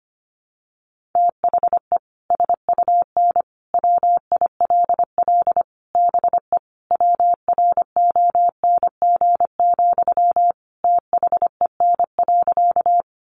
Morse code, heard as a keyed tone, about 25 wpm, 700 Hz. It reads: THE SUN WILL BE WRONG, THEN.
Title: Sentences from Top 900 Words 25wpm